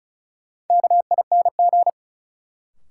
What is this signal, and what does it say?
Morse code, heard as a keyed tone, about 36 wpm, 700 Hz. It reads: KING
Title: Sentences from Top 400 Words - New Word Drill - 35wpm